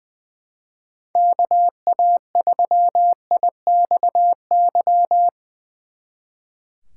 Morse code, heard as a keyed tone, about 23 wpm, 700 Hz. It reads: KA3IXY
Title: Speed Racing - USA 2x3 Call Signs 20wpm